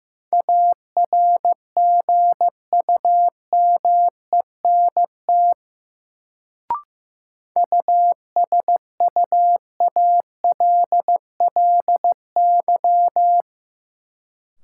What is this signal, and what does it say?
Morse code, heard as a keyed tone, about 15 wpm, 700 Hz. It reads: ARGUMENT USUALLY
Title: Sentences from Top 1400 Words - New Word Drill Alt - 15wpm